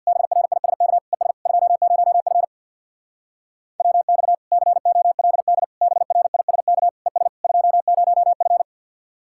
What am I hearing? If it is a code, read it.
Morse code, heard as a keyed tone, about 61 wpm, 700 Hz. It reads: BRISK ES 20F WX COLD BRISK ES 20F
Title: Speed-Racing Common QSO Phrases - 50wpm